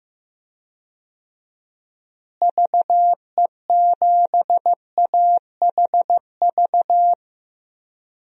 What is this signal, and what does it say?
Morse code, heard as a keyed tone, about 15 wpm, 700 Hz. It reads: VE7AHV
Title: DXCC-100 Call Signs - 6 Characters - Repeated 2x - 15wpm